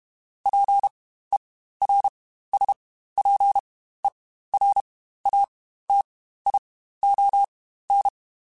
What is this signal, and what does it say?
Morse code, heard as a keyed tone, about 32 wpm, spaced as for 15 wpm, 800 Hz. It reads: PERSPERATION
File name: Quotes Spaced / Quotes Spaced-mp3s / Persperation sp @30